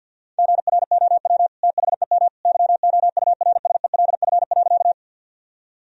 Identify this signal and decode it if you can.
Morse code, heard as a keyed tone, about 50 wpm, 700 Hz. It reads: GROW THEM YOURSELF.
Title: Sentences from Top 1000 Words - Review Part 1 - 50wpm